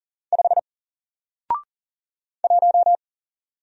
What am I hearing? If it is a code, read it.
Morse code, heard as a keyed tone, about 40 wpm, 700 Hz. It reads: 5 1
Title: Single Letter-Number Alt 40wpm